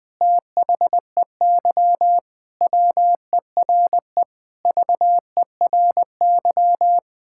Morse code, heard as a keyed tone, about 20 wpm, 700 Hz. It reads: THEY WERE VERY
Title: Sentences with 6-Word Phrases - 20wpm